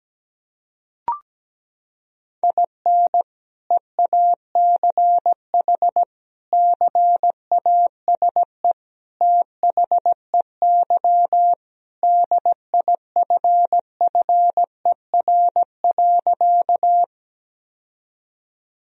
Morse code, heard as a keyed tone, about 17 wpm, 700 Hz. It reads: IN EACH CASE THEY DIFFER.
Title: Sentences from Top 1000 Words - Review Part 1 - 17wpm